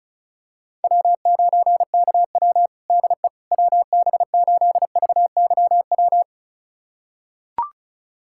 Morse code, heard as a keyed tone, about 36 wpm, 700 Hz. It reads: W9KW DE WB8VYW
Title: USA Callsign DE Callsign 35wpm